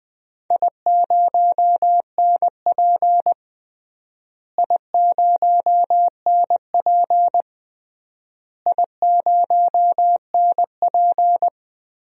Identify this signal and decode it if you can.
Morse code, heard as a keyed tone, about 20 wpm, 700 Hz. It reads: I0NP I0NP I0NP
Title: DXCC-100 Call Signs - 4 Characters - Repeated 3x - 20wpm